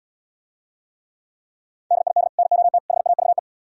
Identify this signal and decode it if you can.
Morse code, heard as a keyed tone, about 79 wpm, 700 Hz. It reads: DID NOT HAVE